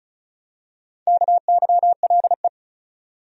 Morse code, heard as a keyed tone, about 36 wpm, 700 Hz. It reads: KYLE
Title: USA Common Names - Max 4 Letters - 35wpm